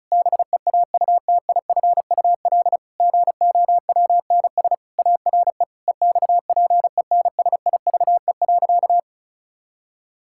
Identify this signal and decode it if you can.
Morse code, heard as a keyed tone, about 36 wpm, 700 Hz. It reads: BEAUTIFUL GOWNS ARE EXPENSIVE.
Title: Pangram Challenge v2 - 35wpm